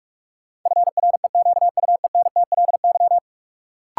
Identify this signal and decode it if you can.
Morse code, heard as a keyed tone, about 46 wpm, 700 Hz. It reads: FREQUENTLY
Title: Sentences from Top 1300 Words - New Word Drill - 45wpm